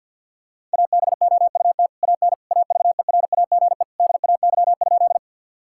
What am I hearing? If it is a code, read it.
Morse code, heard as a keyed tone, about 50 wpm, 700 Hz. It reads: ABOUT AN AVERAGE DAY?
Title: Sentences from Top 1000 Words - Review Part 6 - 50wpm